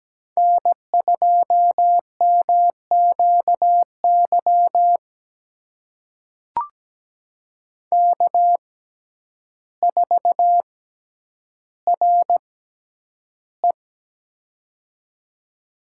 Morse code, heard as a keyed tone, about 17 wpm, spaced as for 9 wpm, 700 Hz. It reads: N2MQY K4RE
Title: USA Call Signs - 6x Character Spacing - 17wpm